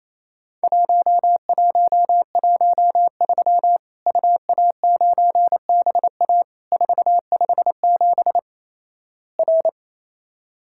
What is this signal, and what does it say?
Morse code, heard as a keyed tone, about 28 wpm, 700 Hz. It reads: 1113 UA9BA 457 R
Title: Worked All Europe DX Contest - QTC Exchange - 28wpm